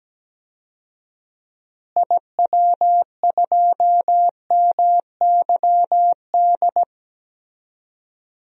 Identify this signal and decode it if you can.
Morse code, heard as a keyed tone, about 17 wpm, 700 Hz. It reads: IW2MYD
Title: DXCC-100 Call Signs - 6 Characters - Repeated 2x - 17wpm